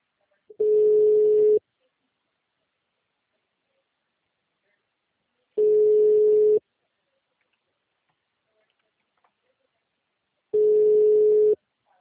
Ringback tone (Europe, 425 Hz).
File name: out- -2008- - - .6